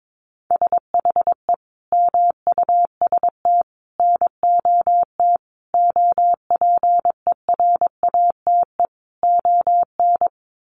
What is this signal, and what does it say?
Morse code, heard as a keyed tone, about 22 wpm, 700 Hz. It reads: SHE MUST NOT OPERATE ON
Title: Sentences from Top 1500 Words 22wpm